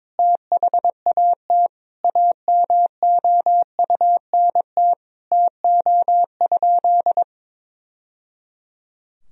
Morse code, heard as a keyed tone, about 22 wpm, 700 Hz. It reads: THAT AMOUNT TO?
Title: Sentences from Top 1000 Words - Review Part 4 - 22wpm